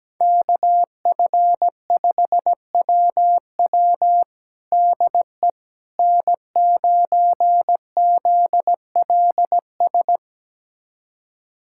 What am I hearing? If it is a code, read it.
Morse code, heard as a keyed tone, about 17 wpm, 700 Hz. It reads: KF5WW DE N9ZLS
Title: USA Callsign DE Callsign - 5 Characters - 17wpm